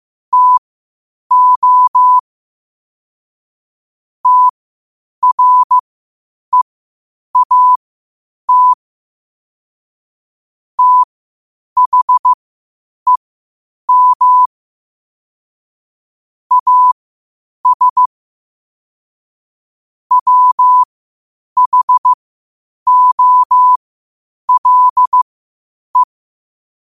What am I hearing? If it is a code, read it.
Morse code, heard as a keyed tone, about 15 wpm, spaced as for 8 wpm, 1000 Hz. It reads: TO TREAT THEM AS WHOLE
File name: qotdfiles / 8wpm